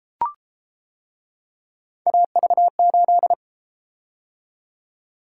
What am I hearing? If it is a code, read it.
Morse code, heard as a keyed tone, about 31 wpm, 700 Hz. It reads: AV8